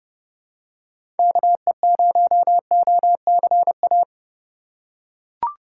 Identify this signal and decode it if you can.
Morse code, heard as a keyed tone, about 30 wpm, 700 Hz. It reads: KE0OCA
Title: USA Call Signs - 6 Characters - 4x Character Spacing - 30wpm